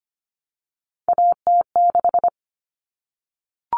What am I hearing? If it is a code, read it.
Morse code, heard as a keyed tone, about 25 wpm, 700 Hz. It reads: AT6
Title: Common Cut Numbers - 3-Digit 25wpm